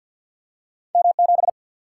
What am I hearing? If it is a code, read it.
Morse code, heard as a keyed tone, about 52 wpm, 700 Hz. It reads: M 7
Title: Single-Letter Single-Number 50wpm